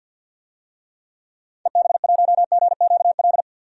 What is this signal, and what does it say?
Morse code, heard as a keyed tone, about 52 wpm, 700 Hz. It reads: EB1GQL
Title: DXCC-100 Call Signs - 6 Characters - 50wpm